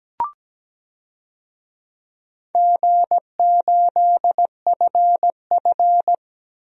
Morse code, heard as a keyed tone, about 17 wpm, 700 Hz. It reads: G8FF
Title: DXCC-100 Call Signs - 4 Characters - Repeated 2x - 17wpm